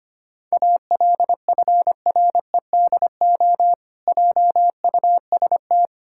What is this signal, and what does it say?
Morse code, heard as a keyed tone, about 25 wpm, 700 Hz. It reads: ALFREDO JUST
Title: Pangram Challenge v2 - 25wpm